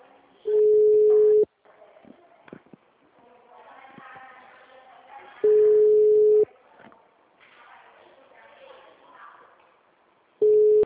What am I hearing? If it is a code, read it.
Ringback tone (Europe, 425 Hz), with speech.